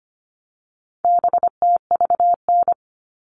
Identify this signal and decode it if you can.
Morse code, heard as a keyed tone, about 25 wpm, 700 Hz. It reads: BTVN
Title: All Cut Numbers - 4-Digit 25wpm